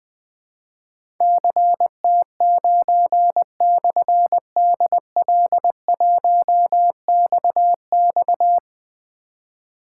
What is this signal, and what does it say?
Morse code, heard as a keyed tone, about 21 wpm, 700 Hz. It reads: CT9/DL1XX
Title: Difficult Call Signs 20wpm